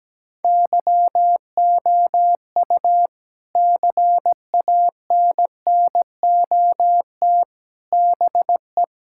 Morse code, heard as a keyed tone, about 17 wpm, 700 Hz. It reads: YOU CANNOT BE
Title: Sentences from Top 1000 Words - Review Part 6 - 17wpm